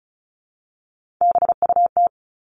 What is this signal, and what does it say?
Morse code, heard as a keyed tone, about 34 wpm, 700 Hz. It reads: BUT
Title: Top 300 Words 35wpm